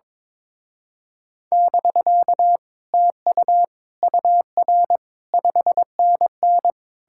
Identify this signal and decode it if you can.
Morse code, heard as a keyed tone, about 22 wpm, 700 Hz. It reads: <BK> TU UR 5NN
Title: Speed-Racing Common QSO Phrases Encore - 22wpm